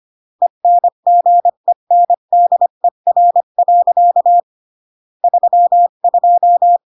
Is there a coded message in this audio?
Yes: Morse code from 0.4 to 6.8 s, heard as a keyed tone, about 25 wpm, 700 Hz. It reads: ENGENDER. 32